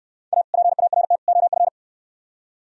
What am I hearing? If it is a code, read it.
Morse code, heard as a keyed tone, about 65 wpm, 700 Hz. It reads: A PART OF